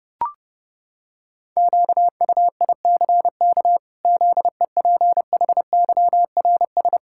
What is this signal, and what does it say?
Morse code, heard as a keyed tone, about 28 wpm, 700 Hz. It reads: QUICK ZEPHYRS
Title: Pangram Challenge v2 - 30wpm